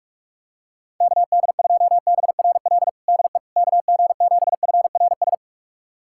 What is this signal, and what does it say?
Morse code, heard as a keyed tone, about 41 wpm, 700 Hz. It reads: KD2BRL DE KG7FRS
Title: USA Callsign DE Callsign 45wpm